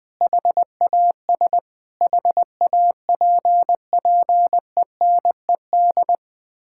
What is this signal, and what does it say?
Morse code, heard as a keyed tone, about 20 wpm, 700 Hz. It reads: HAS HAPPENED
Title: Sets of 8 Words - 20wpm